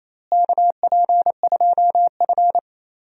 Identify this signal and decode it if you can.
Morse code, heard as a keyed tone, about 28 wpm, 700 Hz. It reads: KP2F
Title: DXCC-100 Call Signs - 4 Characters - Repeated 3x - 28wpm